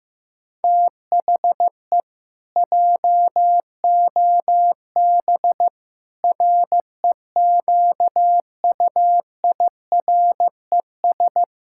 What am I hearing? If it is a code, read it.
Morse code, heard as a keyed tone, about 15 wpm, 700 Hz. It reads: THE JOB REQUIRES
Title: Pangram Challenge v2 - 15wpm